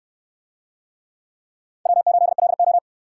Morse code, heard as a keyed tone, about 63 wpm, 700 Hz. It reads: W8LO